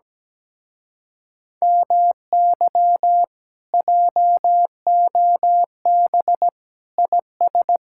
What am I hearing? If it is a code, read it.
Morse code, heard as a keyed tone, about 17 wpm, 700 Hz. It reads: MY JOB IS
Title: Speed-Racing Common QSO Phrases - 17wpm